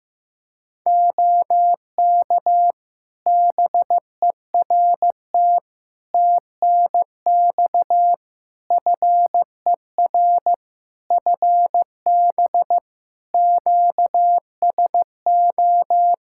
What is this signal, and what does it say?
Morse code, heard as a keyed tone, about 15 wpm, 700 Hz. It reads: OK BERT TNX FER FB QSO
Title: Practice QSO - Exchange 4 - Ending - 15wpm